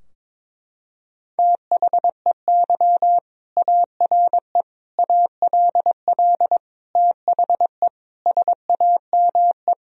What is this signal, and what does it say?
Morse code, heard as a keyed tone, about 22 wpm, 700 Hz. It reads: THEY ARE ALL THE SAME
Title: Sentences from Top 1000 Words - Review Part 4 - 22wpm